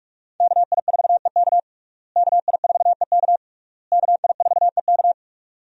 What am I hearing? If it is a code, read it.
Morse code, heard as a keyed tone, about 42 wpm, 700 Hz. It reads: KI4EK KI4EK KI4EK
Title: USA Call Signs - 5 Characters - Repeated 3x - 45wpm